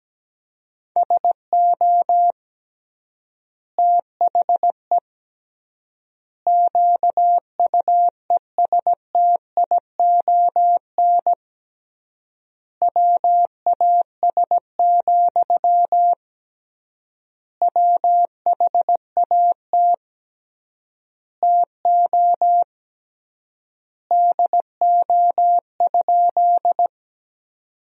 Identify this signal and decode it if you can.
Morse code, heard as a keyed tone, about 17 wpm, 700 Hz. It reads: SO THE QUESTION WAS, WHAT TO DO?
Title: 3x Word-Spacing - Sentences from Top 400 Words - 17wpm